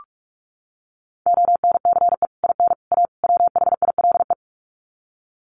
Morse code, heard as a keyed tone, about 44 wpm, 700 Hz. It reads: ONCE IN A WHILE
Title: Sets of 4 Words 45wpm